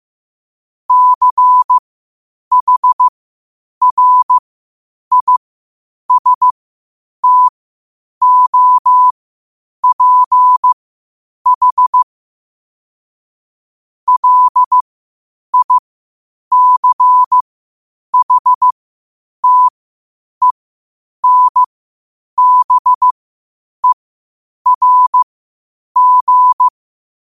Morse code, heard as a keyed tone, about 15 wpm, spaced as for 8 wpm, 1000 Hz. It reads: CHRISTOPH LICHTENBERG
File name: qotdfiles / 8wpm